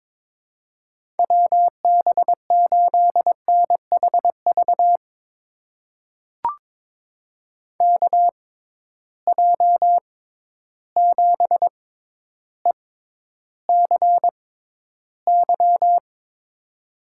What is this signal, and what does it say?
Morse code, heard as a keyed tone, about 22 wpm, spaced as for 11 wpm, 700 Hz. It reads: WB8NHV KJ7ECY
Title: USA Call Signs - 6 Characters - 6x Character Spacing - 22wpm